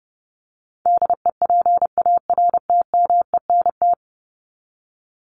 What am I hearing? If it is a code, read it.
Morse code, heard as a keyed tone, about 30 wpm, 700 Hz. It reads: DEPARTMENT
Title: Top 751-1000 Words 30wpm